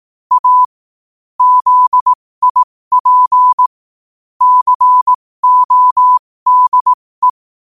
Morse code, heard as a keyed tone, about 18 wpm, spaced as for 15 wpm, 1000 Hz. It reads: A ZIP CODE
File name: qotdfiles / 15wpm